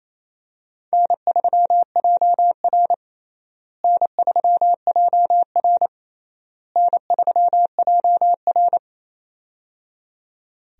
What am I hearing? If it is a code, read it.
Morse code, heard as a keyed tone, about 28 wpm, 700 Hz. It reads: N3JR N3JR N3JR
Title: USA Call Signs - 4 Characters - Repeated 3x - 28wpm